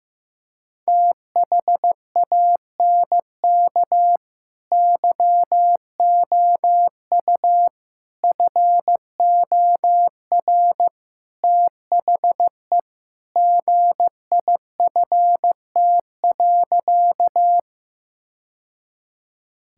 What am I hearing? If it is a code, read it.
Morse code, heard as a keyed tone, about 16 wpm, 700 Hz. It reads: THANK YOU FOR THE GIFT.
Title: Sentences from Top 1000 Words - Review Part 6 - 15wpm